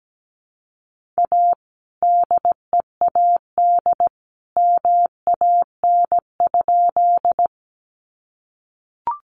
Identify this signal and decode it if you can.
Morse code, heard as a keyed tone, about 17 wpm, 700 Hz. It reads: A DEAD MAN?
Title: Sentences from Top 1500 Words 17wpm